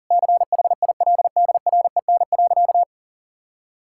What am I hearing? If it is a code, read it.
Morse code, heard as a keyed tone, about 40 wpm, 700 Hz. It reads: CHILDREN.
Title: Sentences from Top 1000 Words - Review Part 5 - 40wpm